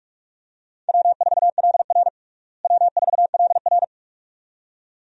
Morse code, heard as a keyed tone, about 46 wpm, 700 Hz. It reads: W4LR W4LR
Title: USA Call Signs - 4 Characters - Repeated 2x - 45wpm